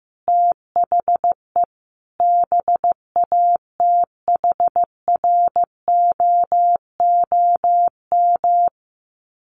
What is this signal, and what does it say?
Morse code, heard as a keyed tone, about 15 wpm, 700 Hz. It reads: THE BATHROOM
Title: Sets of 5 Words Encore 15wpm